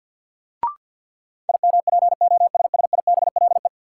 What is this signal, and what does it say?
Morse code, heard as a keyed tone, about 50 wpm, 700 Hz. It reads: IMPOSSIBLE
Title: Sentences from Top 1100 Words - New Word Drill - 50wpm